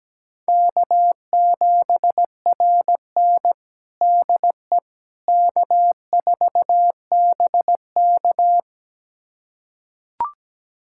Morse code, heard as a keyed tone, about 17 wpm, 700 Hz. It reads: K7RN DE K4BK
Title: USA Callsign DE Callsign - 4 Characters - 17wpm